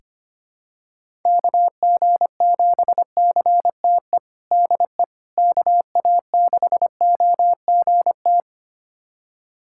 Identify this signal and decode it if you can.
Morse code, heard as a keyed tone, about 25 wpm, 700 Hz. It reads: KG7CTE DE KA6OGT
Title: USA Callsign DE Callsign - 6 Characters - 25wpm